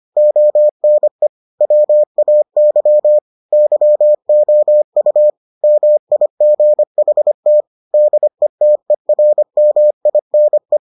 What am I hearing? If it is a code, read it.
Morse code, heard as a keyed tone, about 25 wpm, 600 Hz. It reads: ONE WAY YOU MIGHT DETERMINE